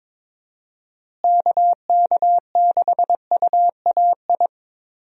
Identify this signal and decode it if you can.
Morse code, heard as a keyed tone, about 22 wpm, 700 Hz. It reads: KK6UAI
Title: USA Call Signs - 6 Characters - Repeated 3x - 22wpm